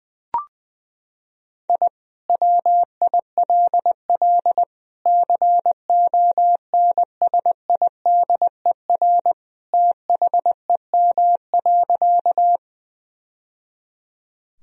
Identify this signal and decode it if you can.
Morse code, heard as a keyed tone, about 21 wpm, 700 Hz. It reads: I WILL CONSIDER THEM.
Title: Sentences from Top 1000 Words - Review Part 4 - 20wpm